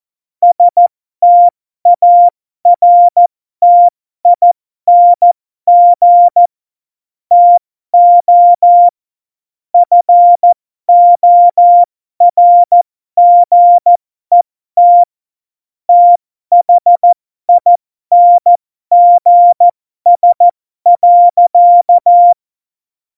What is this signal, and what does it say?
Morse code, heard as a keyed tone, about 14 wpm, spaced as for 12 wpm, 700 Hz. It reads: STARTING TO FORGET THINGS.
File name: Morsekurs / Witze